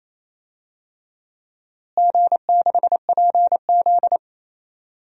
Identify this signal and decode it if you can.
Morse code, heard as a keyed tone, about 28 wpm, 700 Hz. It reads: G6PZ
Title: DXCC-100 Call Signs - 4 Characters - Repeated 3x - 28wpm